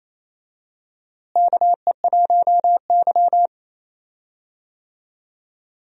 Morse code, heard as a keyed tone, about 28 wpm, 700 Hz. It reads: KE1Y
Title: USA Call Signs - 4 Characters - Repeated 2x - 28wpm